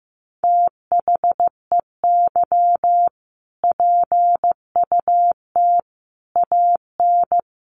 Morse code, heard as a keyed tone, about 15 wpm, 700 Hz. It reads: THEY PUT AN
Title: Sentences from Top 1400 Words 15wpm